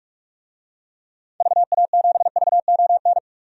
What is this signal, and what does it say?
Morse code, heard as a keyed tone, about 42 wpm, 700 Hz. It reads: VA7VON
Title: DXCC-100 Call Signs - 6 Characters - 45wpm